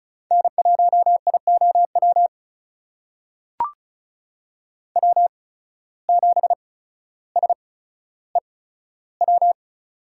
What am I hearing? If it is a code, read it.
Morse code, heard as a keyed tone, about 34 wpm, spaced as for 15 wpm, 700 Hz. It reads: N1IOW W7SEW
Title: USA Call Signs - 5 Characters - 8x Character Spacing - 35wpm